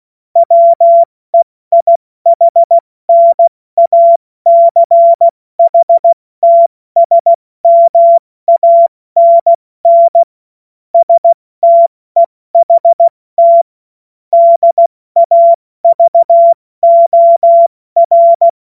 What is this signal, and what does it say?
Morse code, heard as a keyed tone, about 16 wpm, spaced as for 14 wpm, 680 Hz. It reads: WEIHNACHTSMANN STEHT DAVOR
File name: Morsekurs / Witze / Witz 25. April 70 F80